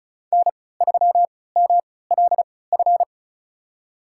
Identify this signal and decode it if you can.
Morse code, heard as a keyed tone, about 36 wpm, spaced as for 20 wpm, 690 Hz. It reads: N3MLF
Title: USA Call Signs - 5 Characters - 3x Character Spacing - 35wpm